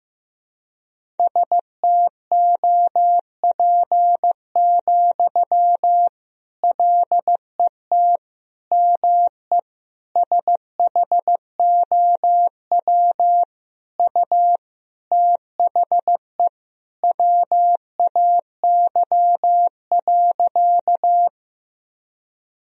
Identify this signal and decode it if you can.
Morse code, heard as a keyed tone, about 15 wpm, 700 Hz. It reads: STOP, LET ME SHOW U THE WAY.
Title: Sentences from Top 300 Words and Abbreviations 15wpm